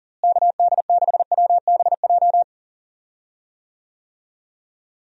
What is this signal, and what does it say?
Morse code, heard as a keyed tone, about 40 wpm, 700 Hz. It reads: KD6WBJ
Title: USA Call Signs - 6 Characters - Repeated 3x - 40wpm